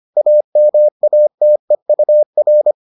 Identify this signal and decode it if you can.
Morse code, heard as a keyed tone, about 25 wpm, 600 Hz. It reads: AMATEUR